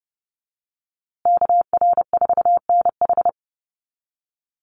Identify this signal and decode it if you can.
Morse code, heard as a keyed tone, about 30 wpm, 700 Hz. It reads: KR4NH
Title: USA Call Signs Repeated 3x 30wpm